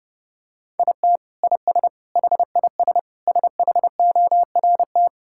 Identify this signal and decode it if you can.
Morse code, heard as a keyed tone, about 32 wpm, 700 Hz. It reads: IT IS HIS SHORT
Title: Sentences from Top 1600 Words 30wpm